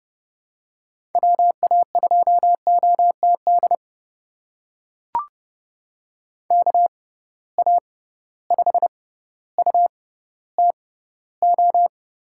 Morse code, heard as a keyed tone, about 30 wpm, spaced as for 15 wpm, 700 Hz. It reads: WA2OTD KA5UTO
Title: USA Call Signs - 6 Characters - 6x Character Spacing - 30wpm